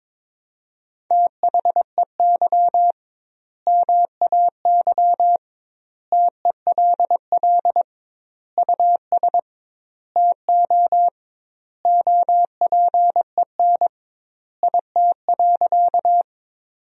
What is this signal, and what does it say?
Morse code, heard as a keyed tone, about 22 wpm, 700 Hz. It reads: THEY MAY TELL US TO OPEN IT.
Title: 2x Word-Spacing - Sentences from Top 300 Words - 22wpm